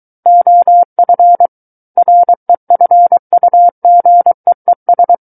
Morse code, heard as a keyed tone, about 23 wpm, 700 Hz. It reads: OF REFUGEES
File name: Book One / 14-Chapter 14 - 23 WPM